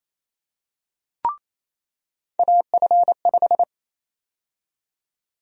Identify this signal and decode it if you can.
Morse code, heard as a keyed tone, about 27 wpm, 700 Hz. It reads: AF5
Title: USA Call Sign Prefixes - 28wpm